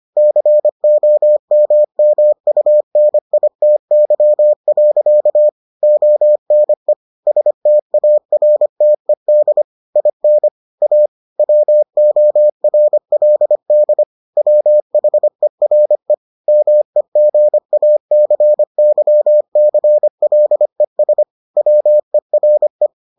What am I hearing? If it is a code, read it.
Morse code, heard as a keyed tone, about 25 wpm, 600 Hz. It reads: COMMUNITY. ONE STARTED IN A WORLD WHERE MEGACYCLES WERE